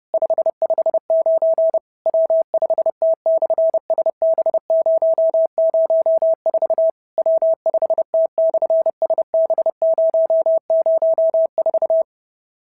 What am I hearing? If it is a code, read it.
Morse code, heard as a keyed tone, about 30 wpm, 650 Hz. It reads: 559 W5T/SB004 W5T/SB004
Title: SOTA QSO Practice 30wpm